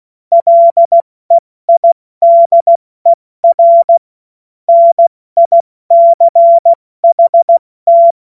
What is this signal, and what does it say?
Morse code, heard as a keyed tone, about 16 wpm, 680 Hz. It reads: LEIDER NICHT
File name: Morsekurs / Witze / Witz 09. Dezember 2025 70 F80